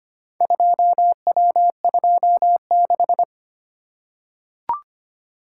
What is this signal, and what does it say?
Morse code, heard as a keyed tone, about 25 wpm, 700 Hz. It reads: 2W26
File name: practice / 25wpm / Four Letter-Number Combinations - 25wpm